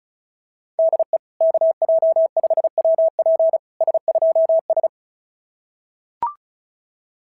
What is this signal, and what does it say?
Morse code, heard as a keyed tone, about 36 wpm, 660 Hz. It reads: DE KJ5WP S2S
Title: SOTA QSO Practice 35wpm